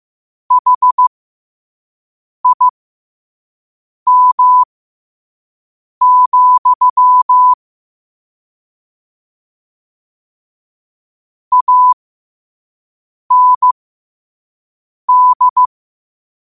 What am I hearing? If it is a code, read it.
Morse code, heard as a keyed tone, about 15 wpm, spaced as for 5 wpm, 1000 Hz. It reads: HIM, AND
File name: qotdfiles / 5wpm